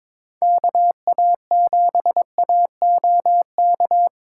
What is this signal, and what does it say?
Morse code, heard as a keyed tone, about 26 wpm, 700 Hz. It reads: KA7AOK
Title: Speed Racing - USA Call Signs 22wpm